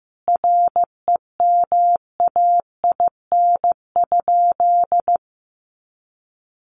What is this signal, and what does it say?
Morse code, heard as a keyed tone, about 15 wpm, 700 Hz. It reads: REMAIN?
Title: Sentences from Top 1200 Words 15wpm